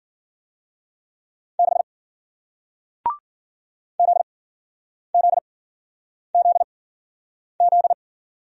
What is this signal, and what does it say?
Morse code, heard as a keyed tone, about 50 wpm, spaced as for 9 wpm, 700 Hz. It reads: 6 ZZZZ
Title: Speed-Racing Single Letter-Number 40wpm